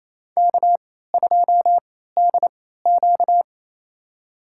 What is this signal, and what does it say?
Morse code, heard as a keyed tone, about 28 wpm, spaced as for 16 wpm, 700 Hz. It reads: K2DQ
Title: USA Call Signs - 4 Characters - 3x Character Spacing - 28wpm